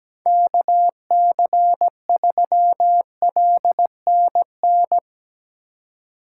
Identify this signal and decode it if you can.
Morse code, heard as a keyed tone, about 17 wpm, 700 Hz. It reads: KC3LNN
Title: USA Call Signs - 6 Characters - 4x Character Spacing - 17wpm